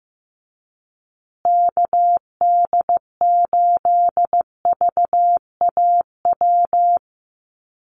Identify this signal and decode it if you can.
Morse code, heard as a keyed tone, about 15 wpm, 700 Hz. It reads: KD8VAW
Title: USA Call Signs Repeated 3x 15wpm